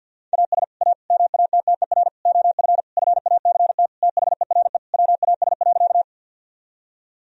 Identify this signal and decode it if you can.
Morse code, heard as a keyed tone, about 50 wpm, 700 Hz. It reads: AS A MATTER OF FACT THERE WAS.
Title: Sentences from Top 1000 Words - Review Part 3 - 50wpm